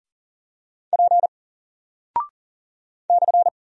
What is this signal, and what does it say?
Morse code, heard as a keyed tone, about 38 wpm, 700 Hz. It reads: P /
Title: Single Character (Letter:Number:Punctuation:Prosign) Alt 40wpm